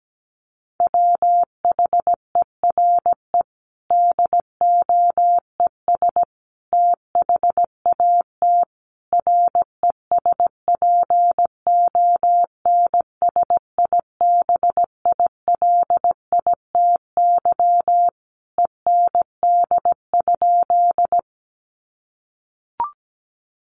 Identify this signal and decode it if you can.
Morse code, heard as a keyed tone, about 17 wpm, 700 Hz. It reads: WHERE DOES THAT RESPONSIBILITY END?
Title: Sentences from Top 1200 Words 17wpm